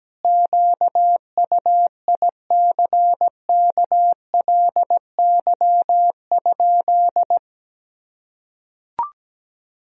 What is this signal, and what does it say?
Morse code, heard as a keyed tone, about 17 wpm, 700 Hz. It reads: QUICKLY?
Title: Sentences from Top 1000 Words - Review Part 3 - 17wpm